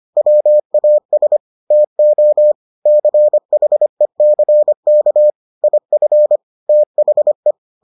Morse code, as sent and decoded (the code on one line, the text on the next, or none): .-- .- ... / - --- / -.-. .... -.-. -.- / .. ..-. / - .... .
WAS TO CHECK IF THE